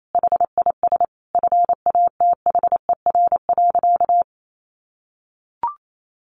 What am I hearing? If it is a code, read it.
Morse code, heard as a keyed tone, about 28 wpm, 700 Hz. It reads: HIS FATHER.
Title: Sentences from Top 1500 Words 28wpm